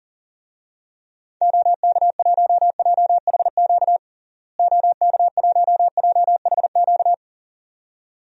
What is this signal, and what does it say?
Morse code, heard as a keyed tone, about 40 wpm, 700 Hz. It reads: OK1JHQ OK1JHQ
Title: DXCC-100 Call Signs - 6 Characters - Repeated 2x - 40wpm